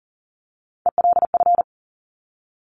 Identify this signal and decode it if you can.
Morse code, heard as a keyed tone, about 41 wpm, 700 Hz. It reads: ELF
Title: Top Three Letter Combinations in Words 40wpm